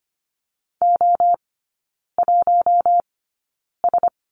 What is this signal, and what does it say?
Morse code, heard as a keyed tone, about 25 wpm, spaced as for 9 wpm, 700 Hz. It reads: O1S
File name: Lesson 15e - Number 5 - Three Characters 25wpm